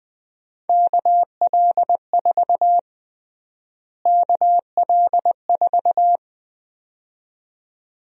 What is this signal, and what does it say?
Morse code, heard as a keyed tone, about 20 wpm, 700 Hz. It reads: KL4 KL4
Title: USA Call Sign Prefixes - Repeated 2x - 20wpm